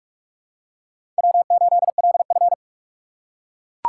Morse code, heard as a keyed tone, about 42 wpm, 700 Hz. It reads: W8LF
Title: USA Call Signs - 4 Characters - 2x Character Spacing - 45wpm